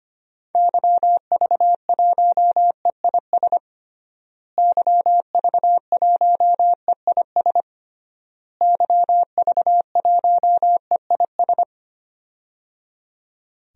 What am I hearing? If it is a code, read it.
Morse code, heard as a keyed tone, about 25 wpm, 700 Hz. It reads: YV1EIS YV1EIS YV1EIS
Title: DXCC-100 Call Signs - 6 Characters - Repeated 3x - 25wpm